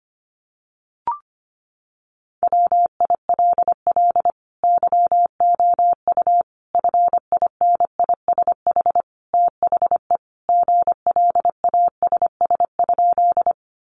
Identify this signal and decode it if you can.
Morse code, heard as a keyed tone, about 25 wpm, 700 Hz. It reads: WILL YOU FINISH THE GLASS?
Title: Sentences from Top 1300 Words 25wpm